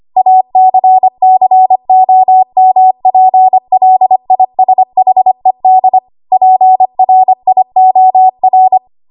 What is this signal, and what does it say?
Morse code, heard as a keyed tone, about 25 wpm, 750 Hz. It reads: ACCOMPLISHED PRIOR
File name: Morse / 25WPM